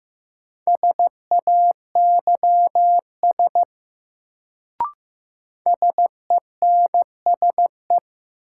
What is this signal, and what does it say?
Morse code, heard as a keyed tone, about 15 wpm, 700 Hz. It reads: SAYS SENSE
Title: Sets of 2 Words - New Word Drill - 15wpm